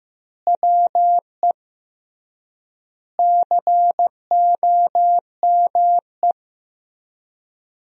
Morse code, heard as a keyed tone, about 15 wpm, 700 Hz. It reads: WE COME
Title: 3x Word-Spacing - Sentences from Top 100 Words - 15wpm